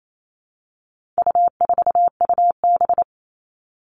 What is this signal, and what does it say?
Morse code, heard as a keyed tone, about 28 wpm, 700 Hz. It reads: U4UB
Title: All Cut Numbers - 4-Digit 28wpm